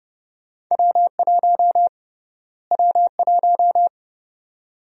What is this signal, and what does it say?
Morse code, heard as a keyed tone, about 30 wpm, 700 Hz. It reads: W1 W1
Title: USA Call Sign Prefixes - Repeated 2x - 30wpm